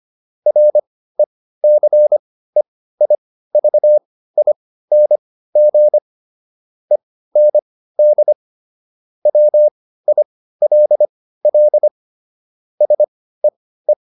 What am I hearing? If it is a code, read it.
Morse code, heard as a keyed tone, about 25 wpm, spaced as for 15 wpm, 600 Hz. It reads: RECEIVING END WILL SEE